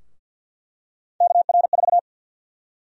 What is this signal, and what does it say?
Morse code, heard as a keyed tone, about 50 wpm, 700 Hz. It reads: KR4
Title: USA Call Sign Prefixes - 50wpm